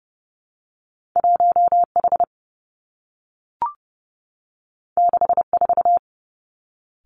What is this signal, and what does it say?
Morse code, heard as a keyed tone, about 30 wpm, 700 Hz. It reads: 1 H 6 4
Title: Two Challenging Letter-Number Combinations 30wpm